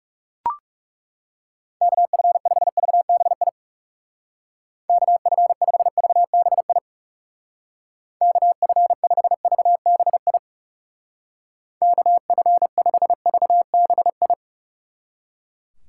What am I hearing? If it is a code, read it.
Morse code, heard as a keyed tone, about 38 wpm, 700 Hz. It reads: KF5VBI KF5VBI KF5VBI KF5VBI